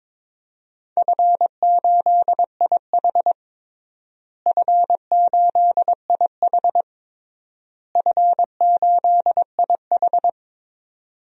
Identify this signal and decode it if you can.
Morse code, heard as a keyed tone, about 22 wpm, 700 Hz. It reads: F8IH F8IH F8IH
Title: DXCC-100 Call Signs - 4 Characters - Repeated 3x - 22wpm